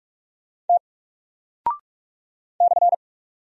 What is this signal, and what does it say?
Morse code, heard as a keyed tone, about 44 wpm, 700 Hz. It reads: T /
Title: Single Character (Letter:Number:Punctuation:Prosign) Alt 45wpm